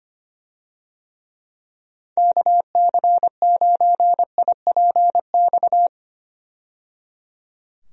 Morse code, heard as a keyed tone, about 25 wpm, 700 Hz. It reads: KC9IPX
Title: USA Call Signs - 6 Characters - 25wpm